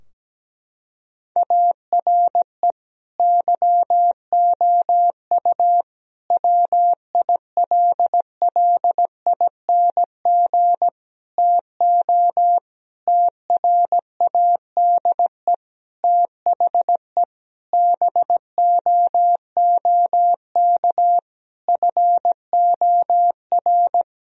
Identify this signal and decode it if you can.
Morse code, heard as a keyed tone, about 17 wpm, 700 Hz. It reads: ARE YOU WILLING TO TRADE THE BOOK FOR
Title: Sentences from Top 1000 Words - Review Part 4 - 17wpm